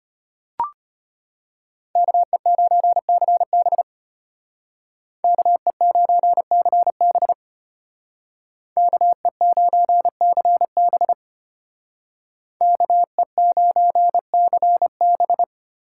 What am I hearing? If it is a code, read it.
Morse code, heard as a keyed tone, about 32 wpm, 700 Hz. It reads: KE9CB KE9CB KE9CB KE9CB